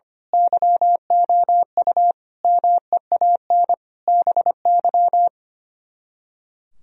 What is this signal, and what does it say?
Morse code, heard as a keyed tone, about 24 wpm, 700 Hz. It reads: YOU MEAN BY
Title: Speed-Racing - Sets of 5 Words 25wpm